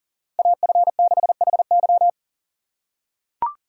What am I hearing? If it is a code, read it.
Morse code, heard as a keyed tone, about 40 wpm, 700 Hz. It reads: AF6HY
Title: USA Call Signs - 3x Character Spacing - 40wpm